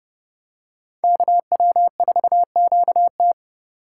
Morse code, heard as a keyed tone, about 29 wpm, 700 Hz. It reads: KW4QT